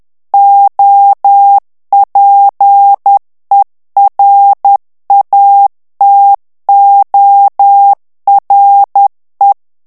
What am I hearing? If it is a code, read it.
Morse code, heard as a keyed tone, about 10 wpm, 800 Hz. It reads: OPERATORE
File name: ITALIEN texte clair 2 vitesse 10 mots minute